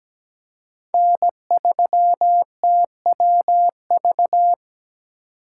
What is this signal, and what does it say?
Morse code, heard as a keyed tone, about 17 wpm, 700 Hz. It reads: N3TWV
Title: USA Call Signs - 5 Characters - 4x Character Spacing - 17wpm